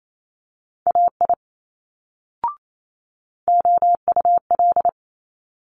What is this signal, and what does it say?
Morse code, heard as a keyed tone, about 28 wpm, 700 Hz. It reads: AI OUL
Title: Multi-letter Phonemes Quick Fire 28wpm